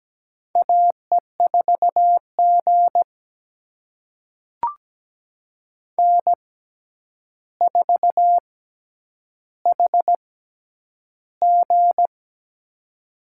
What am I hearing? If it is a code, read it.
Morse code, heard as a keyed tone, about 17 wpm, spaced as for 9 wpm, 700 Hz. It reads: AE4G N4HG
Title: USA Call Signs - 4 Characters - 6x Character Spacing - 17wpm